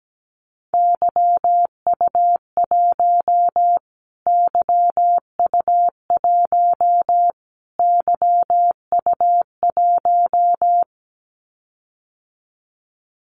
Morse code, heard as a keyed tone, about 17 wpm, 700 Hz. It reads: YU1 YU1 YU1
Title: Callsign Prefixes for Top 25 DXCC - Repeated 3x - 17wpm